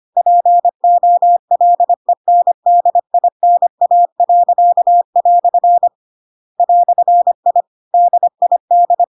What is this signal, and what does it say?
Morse code, heard as a keyed tone, about 25 wpm, 700 Hz. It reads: POLENDINA." "I DID
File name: Pinocchio / Pinocchio ch2 25 25 ik1hge